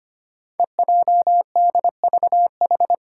Morse code, heard as a keyed tone, about 25 wpm, 700 Hz. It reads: EJDVH
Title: Five Challenging Letter-Number Combinations 25wpm